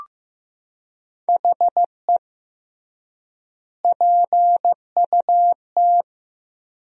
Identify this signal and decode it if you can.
Morse code, heard as a keyed tone, about 16 wpm, 700 Hz. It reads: HE PUT
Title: 3x Word-Spacing - Sentences from Top 400 Words - 15wpm